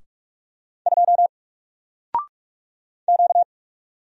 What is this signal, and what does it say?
Morse code, heard as a keyed tone, about 44 wpm, 700 Hz. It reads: 2 Q
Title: Single Letter-Number 45wpm